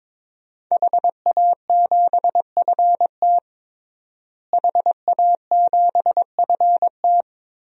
Morse code, heard as a keyed tone, about 21 wpm, 700 Hz. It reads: HA7FT HA7FT
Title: DXCC-100 Call Signs - 5 Characters - Repeated 3x - 22wpm